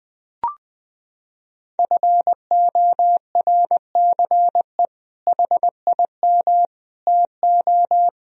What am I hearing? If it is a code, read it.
Morse code, heard as a keyed tone, about 20 wpm, 700 Hz. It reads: FORCE HIM TO
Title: Sentences from Top 1000 Words - Review Part 3 - 20wpm